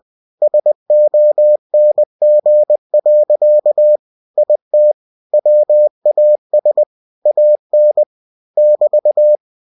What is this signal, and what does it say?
Morse code, heard as a keyed tone, about 20 wpm, 600 Hz. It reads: SONG. IT WAS AN <BT>
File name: cw-books / winnie-the-pooh / winnie-20wpm-page-0103